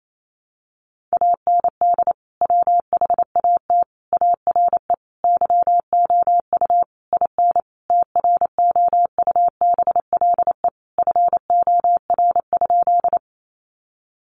Morse code, heard as a keyed tone, about 28 wpm, 700 Hz. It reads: AND WHAT ARE YOU IN TROUBLE FOR?
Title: Sentences from Top 1200 Words 28wpm